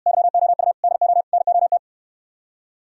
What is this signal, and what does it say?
Morse code, heard as a keyed tone, about 67 wpm, 700 Hz. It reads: YOU DO NOT